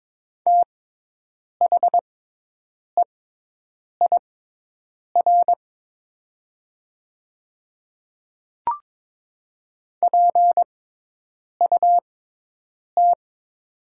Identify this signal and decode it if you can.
Morse code, heard as a keyed tone, about 22 wpm, spaced as for 8 wpm, 700 Hz. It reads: THEIR PUT
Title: 6x Character Spacing - Top 200 Words - 22wpm